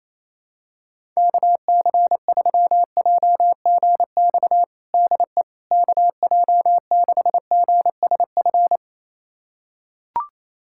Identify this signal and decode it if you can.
Morse code, heard as a keyed tone, about 28 wpm, 700 Hz. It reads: KC3JGX DE KJ6GSF
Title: USA Callsign DE Callsign - 6 Characters - 28wpm